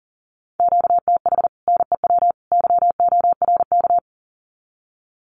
Morse code, heard as a keyed tone, about 37 wpm, 700 Hz. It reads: QTH NEW YORK
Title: Common QSO Phrases 40wpm